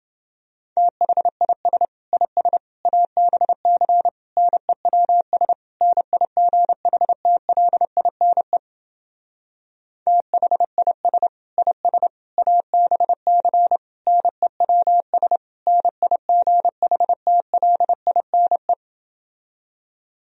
Morse code, heard as a keyed tone, about 28 wpm, 700 Hz. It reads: THIS IS ABC NEWS NIGHTLINE THIS IS ABC NEWS NIGHTLINE